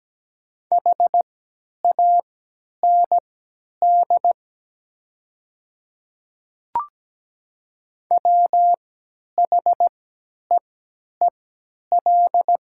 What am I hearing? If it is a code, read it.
Morse code, heard as a keyed tone, about 17 wpm, spaced as for 10 wpm, 700 Hz. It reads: HAND WHEEL
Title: 3x Character Spacing - Top 500 Words - 17wpm